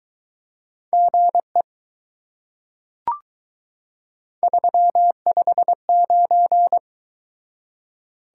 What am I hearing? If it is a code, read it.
Morse code, heard as a keyed tone, about 22 wpm, 700 Hz. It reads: GE 359